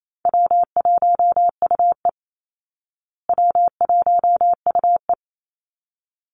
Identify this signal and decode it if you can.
Morse code, heard as a keyed tone, about 28 wpm, 700 Hz. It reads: W1UE W1UE
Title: Easy Call Signs Repeated 2x - 28wpm